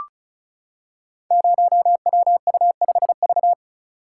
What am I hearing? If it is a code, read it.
Morse code, heard as a keyed tone, about 36 wpm, 700 Hz. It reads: 0WU5V
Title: Code Groups - 5 Letter:Numbers - 35wpm